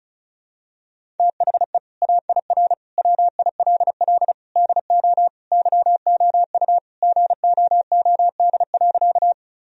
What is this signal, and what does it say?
Morse code, heard as a keyed tone, about 34 wpm, 700 Hz. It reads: THE AIR WILL DO YOU GOOD.
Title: Sentences from Top 1000 Words - Review Part 1 - 35wpm